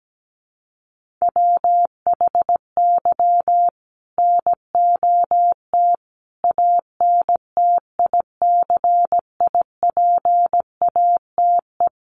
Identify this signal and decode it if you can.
Morse code, heard as a keyed tone, about 17 wpm, 700 Hz. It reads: WHY NOT ANTICIPATE
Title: Sentences from Top 1500 Words 17wpm